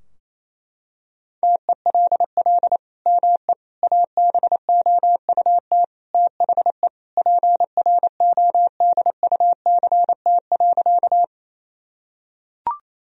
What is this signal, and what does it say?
Morse code, heard as a keyed tone, about 28 wpm, 700 Hz. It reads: TELL ME ABOUT THE PRODUCT.
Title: Sentences from Top 1000 Words - Review Part 2 - 28wpm